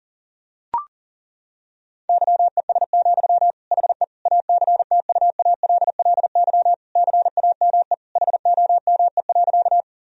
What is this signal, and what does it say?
Morse code, heard as a keyed tone, about 40 wpm, 700 Hz. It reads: YES, HE ACTUALLY CAME HOME.
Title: Sentences from Top 1000 Words - Review Part 4 - 40wpm